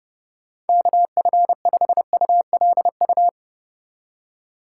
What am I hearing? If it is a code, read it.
Morse code, heard as a keyed tone, about 29 wpm, 700 Hz. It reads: KF5ULU